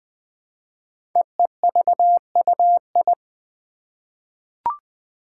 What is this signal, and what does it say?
Morse code, heard as a keyed tone, about 20 wpm, 700 Hz. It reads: EEVUI
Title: Code Groups - 5 Letters - 20wpm